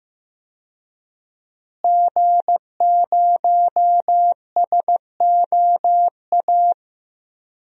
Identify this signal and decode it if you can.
Morse code, heard as a keyed tone, about 15 wpm, 700 Hz. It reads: G0SOA
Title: DXCC-100 Call Signs - 5 Characters - Repeated 3x - 15wpm